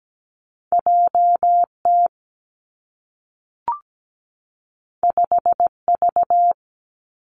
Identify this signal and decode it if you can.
Morse code, heard as a keyed tone, about 17 wpm, 700 Hz. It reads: J T 5 V
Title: Two Challenging Letter-Number Combinations 17wpm